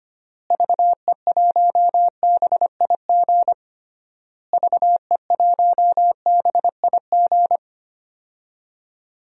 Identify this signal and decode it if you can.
Morse code, heard as a keyed tone, about 25 wpm, 700 Hz. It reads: VE1BIG VE1BIG
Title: DXCC-100 Call Signs - 6 Characters - Repeated 3x - 25wpm